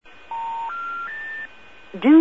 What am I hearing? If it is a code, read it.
Special information tone, with speech.
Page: att-general-error-recordings